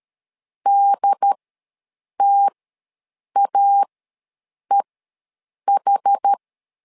Morse code, heard as a keyed tone, about 13 wpm, spaced as for 7 wpm, 800 Hz. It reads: DTAEH
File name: LEZIONI / 2 UHDN / 2 UHDN ws 12wpm 1708